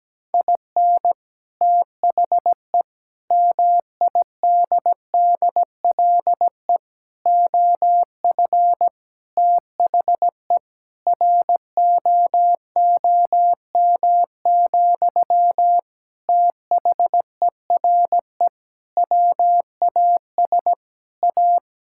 Morse code, heard as a keyed tone, about 17 wpm, 700 Hz. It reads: IN THE MIDDLE OF THE ROOM, THERE WAS A
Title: Sentences from Top 1000 Words - Review Part 6 - 17wpm